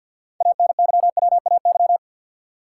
Morse code, heard as a keyed tone, about 51 wpm, 700 Hz. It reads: ANYWAY
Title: Six Letter Words - 50wpm